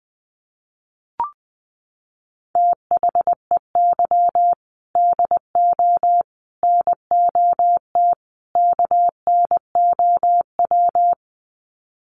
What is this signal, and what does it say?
Morse code, heard as a keyed tone, about 20 wpm, 700 Hz. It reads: THEY DO NOT KNOW
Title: Sets of 4 Words 20wpm